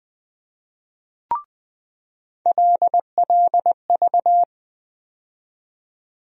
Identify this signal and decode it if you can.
Morse code, heard as a keyed tone, about 21 wpm, 700 Hz. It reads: LLV
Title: USA Call Sign Postfixes - 3 Letters 20wpm